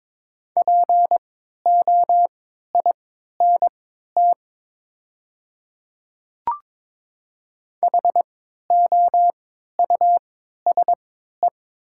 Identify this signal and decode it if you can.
Morse code, heard as a keyed tone, about 23 wpm, spaced as for 13 wpm, 700 Hz. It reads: POINT HOUSE
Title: 3x Character Spacing - Top 200 Words - 22wpm